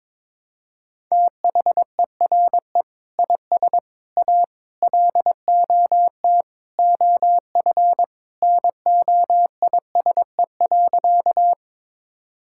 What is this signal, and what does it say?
Morse code, heard as a keyed tone, about 22 wpm, 700 Hz. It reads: THERE IS A LOT OF NOISE.
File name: practice / 22wpm / Sentences with 2-Word Phrases - 22wpm